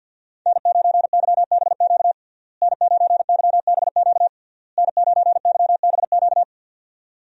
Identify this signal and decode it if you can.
Morse code, heard as a keyed tone, about 50 wpm, 700 Hz. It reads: N9YBQ N9YBQ N9YBQ
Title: USA Call Signs - 5 Characters - Repeated 3x - 50wpm